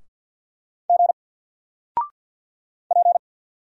Morse code, heard as a keyed tone, about 50 wpm, 700 Hz. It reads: G P
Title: Single Letter-Number 50wpm